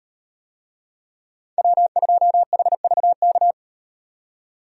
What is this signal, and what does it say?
Morse code, heard as a keyed tone, about 38 wpm, 700 Hz. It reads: W2HVK